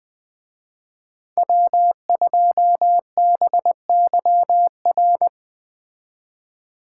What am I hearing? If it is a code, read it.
Morse code, heard as a keyed tone, about 20 wpm, 700 Hz. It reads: W2BYR
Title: Speed Racing - USA 1x3 Call Signs 17wpm